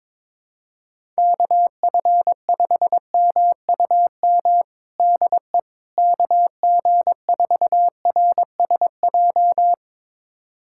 Morse code, heard as a keyed tone, about 22 wpm, 700 Hz. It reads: KF5MUM DE KG4RSJ
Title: USA Callsign DE Callsign - 6 Characters - 22wpm